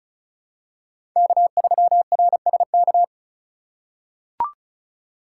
Morse code, heard as a keyed tone, about 36 wpm, 700 Hz. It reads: K3RSK
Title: USA Call Signs - 5 Characters - 4x Character Spacing - 35wpm